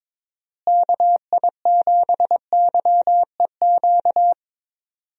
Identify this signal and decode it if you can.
Morse code, heard as a keyed tone, about 22 wpm, 700 Hz. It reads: KI7YEQ
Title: USA Call Signs - 6 Characters - Repeated 3x - 22wpm